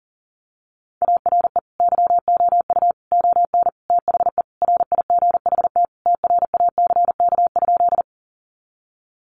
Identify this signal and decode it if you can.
Morse code, heard as a keyed tone, about 40 wpm, 700 Hz. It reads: ARE YOU ON THE RIGHT TRACK?
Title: Sentences from Top 1100 Words 40wpm